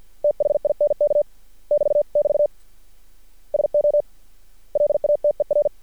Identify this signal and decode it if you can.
Morse code, heard as a keyed tone, about 50 wpm, 590 Hz. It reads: THINK <BT><BT> SO LATER
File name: archived / CRSnet